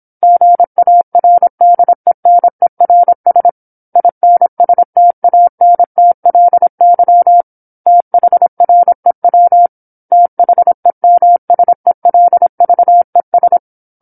Morse code, heard as a keyed tone, about 27 wpm, 700 Hz. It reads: GARDENERS INSTANTLY THREW THEMSELVES
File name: 08-Chapter-08 26 WPM